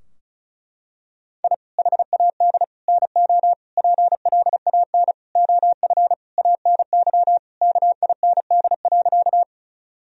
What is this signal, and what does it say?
Morse code, heard as a keyed tone, about 34 wpm, 700 Hz. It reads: I HAD NO PLAN OF ANY KIND.
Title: Sentences from Top 1000 Words - Review Part 2 - 35wpm